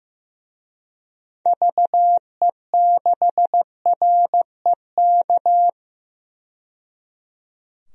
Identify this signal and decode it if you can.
Morse code, heard as a keyed tone, about 15 wpm, 700 Hz. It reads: VE6REK
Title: DXCC-100 Call Signs - 6 Characters - 15wpm